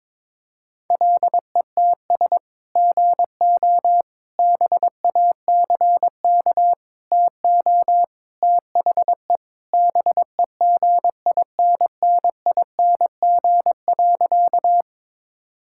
Morse code, heard as a keyed tone, about 22 wpm, 700 Hz. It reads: LETS GO BACK TO THE BEGINNING.
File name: practice / 22wpm / Sentences with 3-Word Phrases - 22wpm